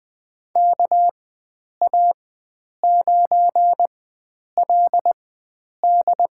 Morse code, heard as a keyed tone, about 21 wpm, spaced as for 9 wpm, 700 Hz. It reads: KA9LD
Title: USA Call Signs - 4x Character Spacing - 20wpm